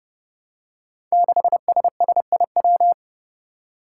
Morse code, heard as a keyed tone, about 30 wpm, 700 Hz. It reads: 6SSIW
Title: Code Groups - 5 Letter:Numbers - 30wpm